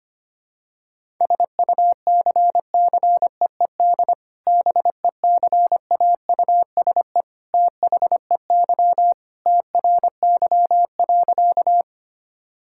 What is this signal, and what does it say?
Morse code, heard as a keyed tone, about 25 wpm, 700 Hz. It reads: SUCCEED BECAUSE THEY TRY.
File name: practice / 25wpm / Sentences with 2-Word Phrases - 25wpm